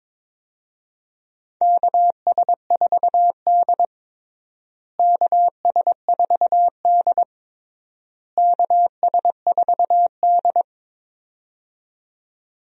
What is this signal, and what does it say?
Morse code, heard as a keyed tone, about 22 wpm, 700 Hz. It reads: KS4D KS4D KS4D
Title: USA Call Signs - 4 Characters - Repeated 3x - 22wpm